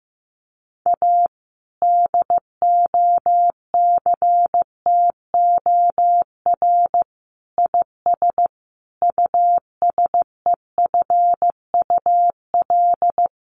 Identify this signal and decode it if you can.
Morse code, heard as a keyed tone, about 15 wpm, 700 Hz. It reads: A DOCTOR IS USEFUL
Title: Sentences from Top 1500 Words 15wpm